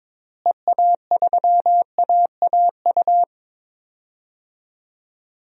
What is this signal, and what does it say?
Morse code, heard as a keyed tone, about 22 wpm, 700 Hz. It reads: EA3AAU
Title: DXCC-100 Call Signs - 6 Characters - Repeated 3x - 22wpm